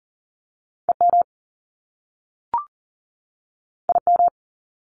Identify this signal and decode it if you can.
Morse code, heard as a keyed tone, about 39 wpm, 700 Hz. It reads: EM IM
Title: Top Two Letter Combinations in Words 40wpm